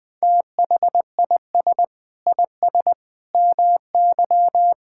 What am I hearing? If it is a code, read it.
Morse code, heard as a keyed tone, about 19 wpm, 700 Hz. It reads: THIS IS MY
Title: Sentences from Top 1000 Words - Review Part 5 - 20wpm